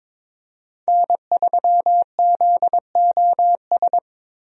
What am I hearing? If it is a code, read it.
Morse code, heard as a keyed tone, about 22 wpm, 700 Hz. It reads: N3ZOS
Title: USA Call Signs - 5 Characters - Repeated 2x - 22wpm